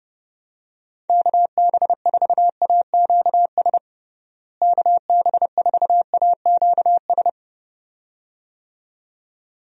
Morse code, heard as a keyed tone, about 32 wpm, 700 Hz. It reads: KB4AQS KB4AQS
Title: USA Call Signs - 6 Characters - Repeated 2x - 30wpm